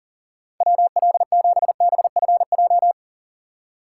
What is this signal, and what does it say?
Morse code, heard as a keyed tone, about 54 wpm, spaced as for 47 wpm, 700 Hz. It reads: WL7BFJ
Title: Speed Racing - USA 2x3 Call Signs 40wpm